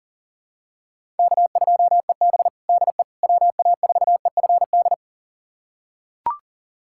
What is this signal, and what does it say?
Morse code, heard as a keyed tone, about 40 wpm, 700 Hz. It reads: K2EB DE WA4EFD
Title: USA Callsign DE Callsign 40wpm